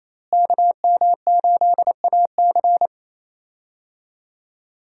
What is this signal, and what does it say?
Morse code, heard as a keyed tone, about 28 wpm, 700 Hz. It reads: KM8AC
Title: USA Call Signs - 5 Characters - Repeated 3x - 28wpm